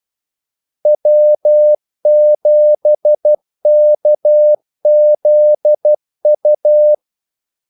Morse code, heard as a keyed tone, about 12 wpm, 600 Hz. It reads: W7KZU
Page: 2 ekstra ordmellemrum mellem kaldesignalerne